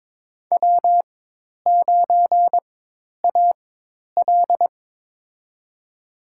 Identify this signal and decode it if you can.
Morse code, heard as a keyed tone, about 22 wpm, spaced as for 10 wpm, 700 Hz. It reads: W9AL
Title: USA Call Signs - 4 Characters - 4x Character Spacing - 22wpm